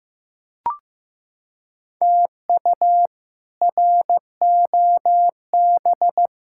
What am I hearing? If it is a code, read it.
Morse code, heard as a keyed tone, about 15 wpm, 700 Hz. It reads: TU ROB
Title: K1USN SST QSO Practice - 15wpm